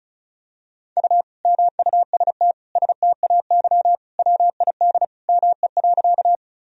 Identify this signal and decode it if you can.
Morse code, heard as a keyed tone, about 36 wpm, 700 Hz. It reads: U MUST STAY WID ME.
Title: Sentences from Top 500 Words and Abbreviations 35wpm